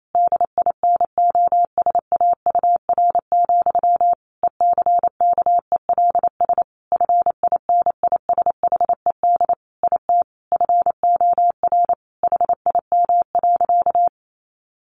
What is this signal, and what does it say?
Morse code, heard as a keyed tone, about 28 wpm, 700 Hz. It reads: DINOSAUR, ECKELS FINISHED IT FOR HIM.
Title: A Sound Of Thunder 28wpm